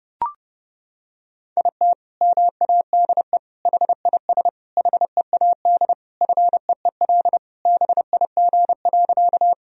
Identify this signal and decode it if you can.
Morse code, heard as a keyed tone, about 30 wpm, 700 Hz. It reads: IT MADE HIS HEAD FEEL BIG.
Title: Sentences from Top 1000 Words - Review Part 1 - 30wpm